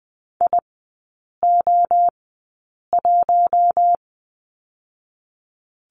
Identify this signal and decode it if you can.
Morse code, heard as a keyed tone, about 19 wpm, spaced as for 8 wpm, 700 Hz. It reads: IO1
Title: 121 Lesson 20e - Number 3 - Three Characters 20wpm